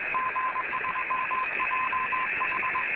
Morse code, heard as a keyed tone, about 20 wpm, 1030 Hz, with radio noise. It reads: G3JH